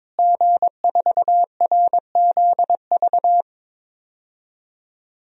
Morse code, heard as a keyed tone, about 22 wpm, 700 Hz. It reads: G4RZV
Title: DXCC-100 Call Signs - 5 Characters - Repeated 3x - 22wpm